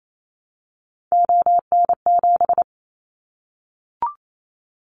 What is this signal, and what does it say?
Morse code, heard as a keyed tone, about 28 wpm, 700 Hz. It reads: ON7
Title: Callsign Prefixes for Top 50 DXCC - 28wpm